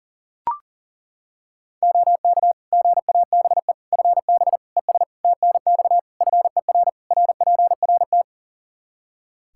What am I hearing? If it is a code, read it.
Morse code, heard as a keyed tone, about 40 wpm, 700 Hz. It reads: OK GABE FB ES TNX FER RPRT
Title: Practice QSO - Exchange 2 - Info - 40wpm